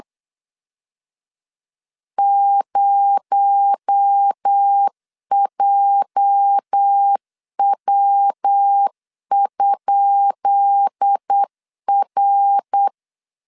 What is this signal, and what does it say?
Morse code, heard as a keyed tone, about 8 wpm, 800 Hz. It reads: 0JW?R
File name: LEZIONI / 10 SIGNS / 10B INT 8wpm 3534